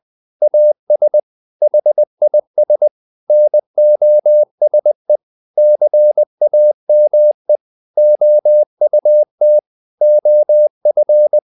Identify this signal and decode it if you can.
Morse code, heard as a keyed tone, about 20 wpm, 600 Hz. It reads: AS HIS NOSE CAME OUT OF
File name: the old man and the sea part 2 20wpm